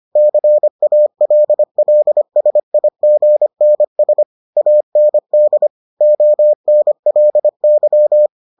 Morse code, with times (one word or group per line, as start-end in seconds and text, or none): 0.2-4.2 CALLSIGNS
4.6-5.7 AND
6.0-8.3 ONLY